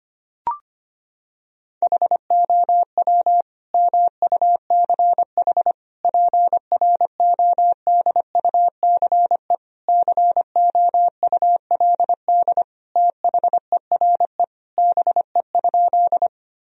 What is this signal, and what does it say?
Morse code, heard as a keyed tone, about 25 wpm, 700 Hz. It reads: HOW MUCH PRODUCE COULD THERE BE?
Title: Sentences from Top 1000 Words - Review Part 3 - 25wpm